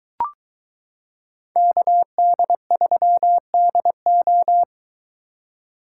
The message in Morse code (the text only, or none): KD3DO